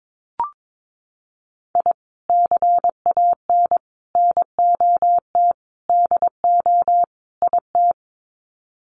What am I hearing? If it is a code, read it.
Morse code, heard as a keyed tone, about 22 wpm, 700 Hz. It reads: I CAN NOT DO IT
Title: Sets of 5 Words Encore 22wpm